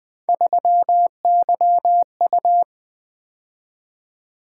Morse code, heard as a keyed tone, about 20 wpm, 700 Hz. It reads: 3YU
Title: Three Letter-Number Combinations - 20wpm